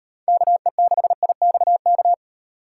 Morse code, heard as a keyed tone, about 37 wpm, 700 Hz. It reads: KE6IXK